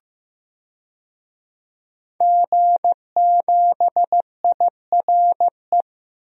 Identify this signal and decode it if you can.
Morse code, heard as a keyed tone, about 16 wpm, 700 Hz. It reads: G7IRE
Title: DXCC-100 Call Signs - 5 Characters - Repeated 2x - 15wpm